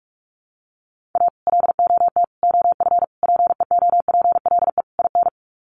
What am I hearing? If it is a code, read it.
Morse code, heard as a keyed tone, about 44 wpm, 700 Hz. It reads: A LOT OF PEOPLE IN
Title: Sets of 5 Words 45wpm